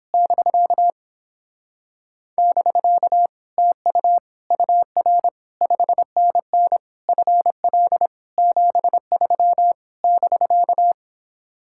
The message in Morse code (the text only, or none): <BK> <BK> TU UR 5NN FL 73 <BK>